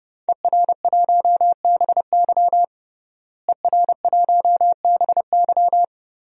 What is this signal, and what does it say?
Morse code, heard as a keyed tone, about 32 wpm, 700 Hz. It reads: ER1BY ER1BY
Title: DXCC-100 Call Signs - 5 Characters - Repeated 2x - 30wpm